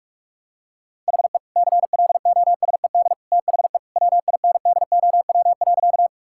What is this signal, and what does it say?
Morse code, heard as a keyed tone, about 46 wpm, 700 Hz. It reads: HE CLOSED THE WINDOW.
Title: Sentences from Top 1600 Words 45wpm